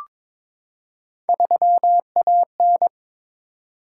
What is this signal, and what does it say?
Morse code, heard as a keyed tone, about 22 wpm, 700 Hz. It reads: 3AN
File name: practice / 22wpm / Three Letter-Number Combinations - 22wpm